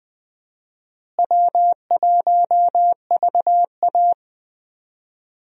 Morse code, heard as a keyed tone, about 20 wpm, 700 Hz. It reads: W1VA
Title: USA Call Signs - 4 Characters - Repeated 3x - 20wpm